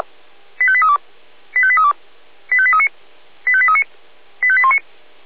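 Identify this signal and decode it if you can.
Five-tone selective call (ZVEI), heard as quick runs of five tones, tones of about 70 ms. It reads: 85832 85832 85839 85839 85819
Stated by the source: Five-tone ZVEI selective call received on an SDR.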